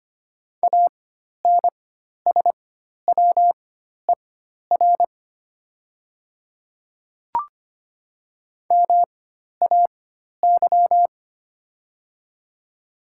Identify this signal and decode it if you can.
Morse code, heard as a keyed tone, about 25 wpm, spaced as for 12 wpm, 700 Hz. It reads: ANSWER MAY
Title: 4x Character Spacing - Top 300 Words - 25wpm